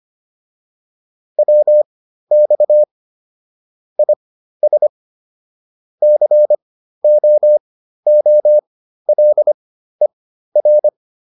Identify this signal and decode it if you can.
Morse code, heard as a keyed tone, about 25 wpm, spaced as for 13 wpm, 600 Hz. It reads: WX IS COOLER
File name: QSO409 13